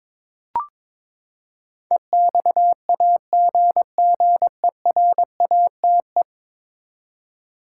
Morse code, heard as a keyed tone, about 22 wpm, 700 Hz. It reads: EXAGGERATE
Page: morse-code-ninja-practice-22wpm